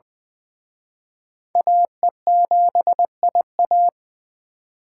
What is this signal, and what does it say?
Morse code, heard as a keyed tone, about 25 wpm, 700 Hz. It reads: AE7IA
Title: Speed Racing - USA 2x2 Call Signs 20wpm